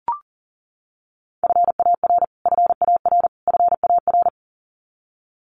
Morse code, heard as a keyed tone, about 40 wpm, 700 Hz. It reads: FAR FAR FAR
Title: Top 500 Words Repeated 40wpm